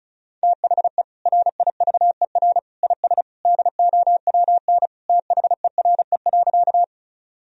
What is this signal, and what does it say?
Morse code, heard as a keyed tone, about 34 wpm, 700 Hz. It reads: THE RIVER IS DOWN THERE.
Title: Sentences from Top 1000 Words - Review Part 1 - 35wpm